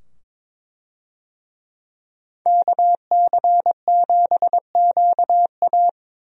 Morse code, heard as a keyed tone, about 22 wpm, 700 Hz. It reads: KC7QA
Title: USA Call Signs - 5 Characters - 22wpm